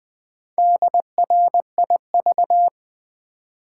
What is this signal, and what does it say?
Morse code, heard as a keyed tone, about 20 wpm, 700 Hz. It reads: DRIV
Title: Build A Word - 1 Letter At A Time - 3 to 5-Letter Words - 20wpm